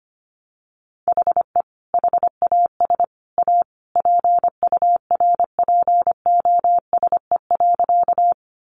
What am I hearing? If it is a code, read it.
Morse code, heard as a keyed tone, about 25 wpm, 700 Hz. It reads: HE HAS A PURPOSE.
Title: Sentences from Top 800 Words 25wpm